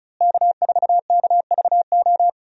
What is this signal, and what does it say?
Morse code, heard as a keyed tone, about 36 wpm, 700 Hz. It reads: K4KVO
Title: USA Call Signs - 5 Characters - Repeated 3x - 35wpm